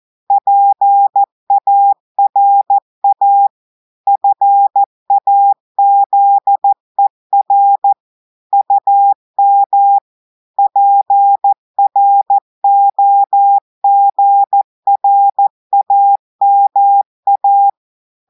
Morse code, heard as a keyed tone, about 14 wpm, 800 Hz. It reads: PARA FAZER UM PROGRAMA
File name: audios / portugais / portugais texte en clair 2 vitesse 14 mots minutes